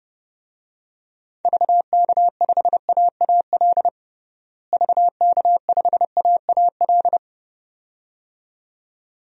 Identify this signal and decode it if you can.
Morse code, heard as a keyed tone, about 30 wpm, 700 Hz. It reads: VK5AAL VK5AAL
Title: DXCC-100 Call Signs - 6 Characters - Repeated 2x - 30wpm